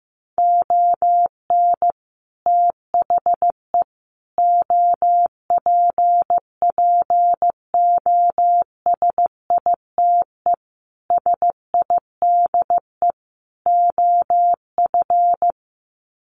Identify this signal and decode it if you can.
Morse code, heard as a keyed tone, about 15 wpm, 700 Hz. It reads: ON THE OPPOSITE SIDE OF
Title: Sets of 5 Words Encore2 15wpm